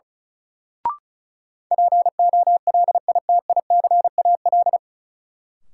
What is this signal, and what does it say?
Morse code, heard as a keyed tone, about 36 wpm, 700 Hz. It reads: POLITICAL
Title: Sentences from Top 800 Words - New Word Drill - 35wpm